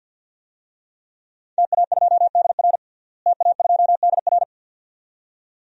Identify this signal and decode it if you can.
Morse code, heard as a keyed tone, about 50 wpm, 700 Hz. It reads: TA2DR TA2DR
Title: DXCC-100 Call Signs - 5 Characters - Repeated 2x - 50wpm